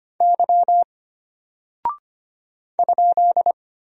Morse code, heard as a keyed tone, about 25 wpm, 700 Hz. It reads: Y ?
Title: Single Character (Letter:Number:Punctuation:Prosign) Alt 25wpm